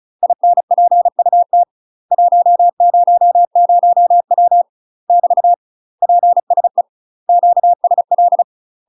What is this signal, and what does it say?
Morse code, heard as a keyed tone, about 36 wpm, 700 Hz. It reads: INPUT 100W <BT> PSE QSL
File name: UK MASTER QSOs TRAINING / 35WPM / UK MASTER QSOs 0010